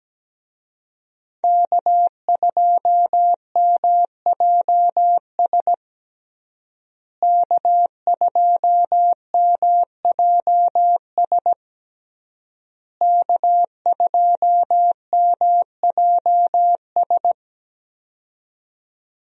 Morse code, heard as a keyed tone, about 17 wpm, 700 Hz. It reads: K2MJS K2MJS K2MJS
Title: USA Call Signs - 5 Characters - Repeated 3x - 17wpm